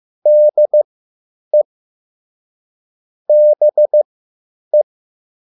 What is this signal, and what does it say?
Morse code, heard as a keyed tone, about 15 wpm, spaced as for 9 wpm, 600 Hz. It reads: DE BE